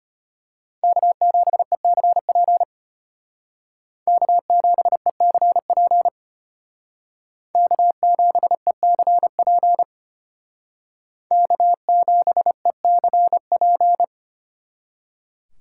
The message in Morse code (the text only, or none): K7ECP K7ECP K7ECP K7ECP